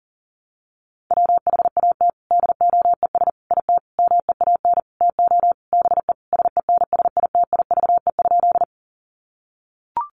Morse code, heard as a keyed tone, about 40 wpm, 700 Hz. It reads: WHAT DOES IT MEAN TO BE SENSITIVE?
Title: Sentences from Top 1400 Words 40wpm